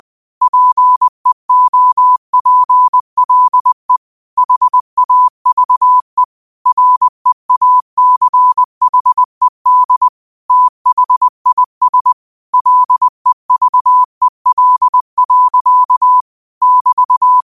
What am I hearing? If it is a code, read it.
Morse code, heard as a keyed tone, about 20 wpm, 1000 Hz. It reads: PEOPLE HAVE REACHED THIS LEVEL. <BT>